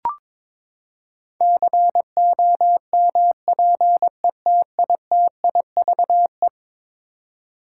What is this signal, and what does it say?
Morse code, heard as a keyed tone, about 22 wpm, 700 Hz. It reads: COMPETITIVE
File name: practice / 22wpm / Eleven Letter Words - 22wpm